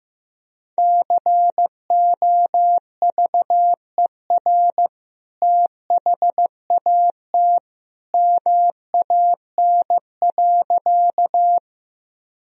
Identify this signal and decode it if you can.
Morse code, heard as a keyed tone, about 16 wpm, 700 Hz. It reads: COVER THAT MAN.
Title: Sentences from Top 1000 Words - Review Part 1 - 15wpm